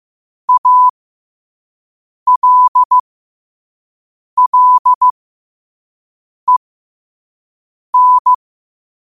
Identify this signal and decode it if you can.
Morse code, heard as a keyed tone, about 15 wpm, spaced as for 5 wpm, 1000 Hz. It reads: ALLEN
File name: qotdfiles / 5wpm